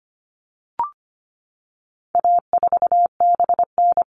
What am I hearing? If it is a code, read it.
Morse code, heard as a keyed tone, about 25 wpm, 700 Hz. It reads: A4BN
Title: All Cut Numbers - 4-Digit 25wpm